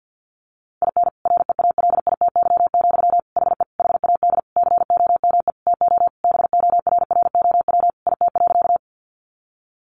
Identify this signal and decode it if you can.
Morse code, heard as a keyed tone, about 50 wpm, 700 Hz. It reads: IN REALITY, HE HAD COME TO BORROW IT.
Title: Sentences from Top 1500 Words 50wpm